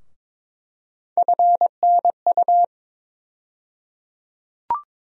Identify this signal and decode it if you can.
Morse code, heard as a keyed tone, about 21 wpm, 700 Hz. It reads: FNU
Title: USA Call Sign Postfixes - 3 Letters 22wpm